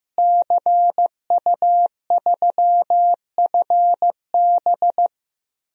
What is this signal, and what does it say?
Morse code, heard as a keyed tone, about 15 wpm, 700 Hz. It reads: CU3FB
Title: DXCC-100 Call Signs - 5 Characters - Repeated 2x - 15wpm